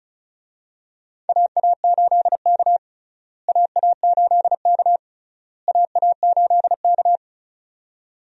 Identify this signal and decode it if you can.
Morse code, heard as a keyed tone, about 36 wpm, 700 Hz. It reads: AA8K AA8K AA8K
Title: USA Call Signs - 4 Characters - Repeated 3x - 35wpm